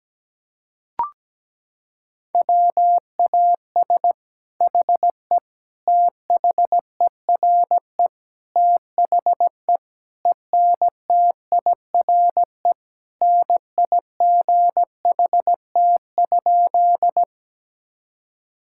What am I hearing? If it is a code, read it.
Morse code, heard as a keyed tone, about 17 wpm, 700 Hz. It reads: WAS HE THERE THE ENTIRE NIGHT?
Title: Sentences from Top 1000 Words - Review Part 5 - 17wpm